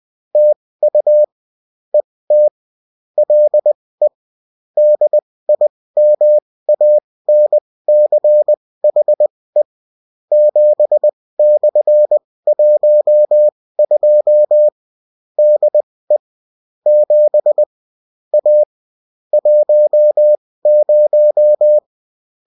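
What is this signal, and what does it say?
Morse code, heard as a keyed tone, about 20 wpm, spaced as for 16 wpm, 600 Hz. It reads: TU ET LE DIMANCHE 7/12 DE 7 A 10
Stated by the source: Bulletin in text and CW form.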